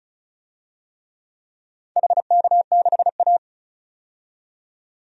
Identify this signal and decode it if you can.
Morse code, heard as a keyed tone, about 36 wpm, 700 Hz. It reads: HK6A
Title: DXCC-100 Call Signs - 4 Characters - 35wpm